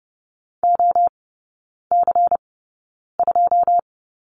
Morse code, heard as a keyed tone, about 30 wpm, spaced as for 9 wpm, 690 Hz. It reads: OC2
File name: Lesson 17e - Letter u - Three Characters 30wpm